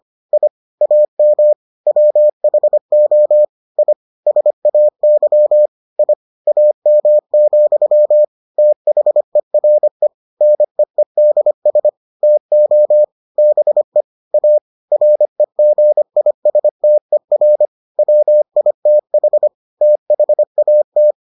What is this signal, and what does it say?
Morse code, heard as a keyed tone, about 25 wpm, 600 Hz. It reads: I AM WHO I SAY I AM, THERE NEEDS TO BE A REGISTER WITH THAT